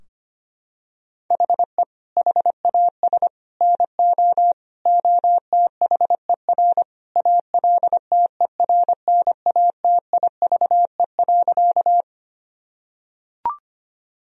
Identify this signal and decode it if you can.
Morse code, heard as a keyed tone, about 25 wpm, 700 Hz. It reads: HE HAS NO OTHER ALTERNATIVE.
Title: Sentences from Top 1000 Words - Review Part 4 - 25wpm